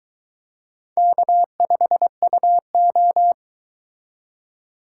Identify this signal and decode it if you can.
Morse code, heard as a keyed tone, about 22 wpm, 700 Hz. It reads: K5UO